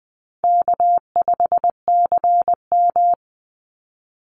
Morse code, heard as a keyed tone, about 20 wpm, 700 Hz. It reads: K5CM
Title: Easy Call Signs Repeated 3x - 20wpm